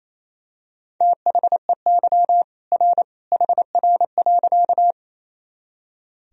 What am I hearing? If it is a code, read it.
Morse code, heard as a keyed tone, about 28 wpm, 700 Hz. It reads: THEY R HR.
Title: Sentences from Top 200 Words and Abbreviations 28wpm